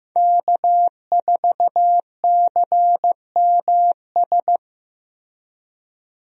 Morse code, heard as a keyed tone, about 15 wpm, 700 Hz. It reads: K4CMS
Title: USA Call Signs - 5 Characters - Repeated 2x - 15wpm